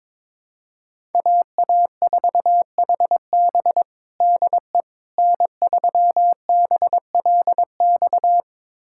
Morse code, heard as a keyed tone, about 22 wpm, 700 Hz. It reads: AA4HB DE N3BLX
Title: USA Callsign DE Callsign - 5 Characters - 22wpm